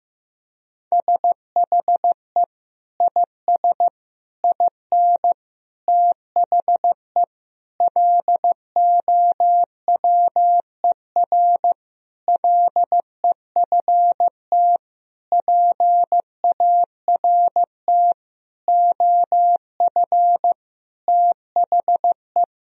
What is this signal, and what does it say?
Morse code, heard as a keyed tone, about 15 wpm, 700 Hz. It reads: SHE IS IN THE LOWER LEFT PART OF THE
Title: Sentences from Top 1000 Words - Review Part 4 - 15wpm